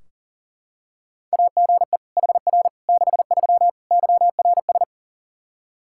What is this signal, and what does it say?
Morse code, heard as a keyed tone, about 40 wpm, 700 Hz. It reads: AGE HR 63 YRS
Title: Practice QSO - Exchange 3 - More Info - 40wpm